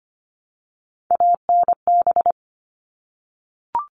Morse code, heard as a keyed tone, about 25 wpm, 700 Hz. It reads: ANB
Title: All Cut Numbers - 3-Digit 25wpm